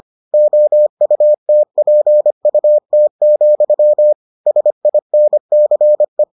Morse code, heard as a keyed tone, about 25 wpm, 600 Hz. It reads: OUTPUT, SINCE